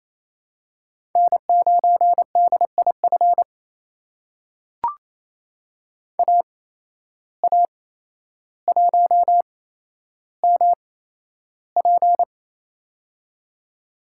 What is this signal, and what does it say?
Morse code, heard as a keyed tone, about 28 wpm, spaced as for 12 wpm, 700 Hz. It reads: N9DIF AA1MP
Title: USA Call Signs - 5 Characters - 8x Character Spacing - 28wpm